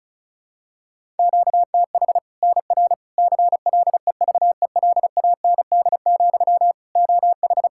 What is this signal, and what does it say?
Morse code, heard as a keyed tone, about 36 wpm, 700 Hz. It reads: QTH NR CLEVELAND, OH
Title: K1USN SST QSO Practice 35wpm